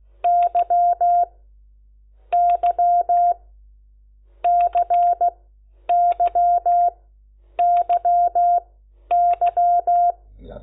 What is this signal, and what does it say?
Morse code, heard as a keyed tone, about 16 wpm, spaced as for 10 wpm, 700 Hz. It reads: Y Y CYYY
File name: Lecciones / CLASE Nº10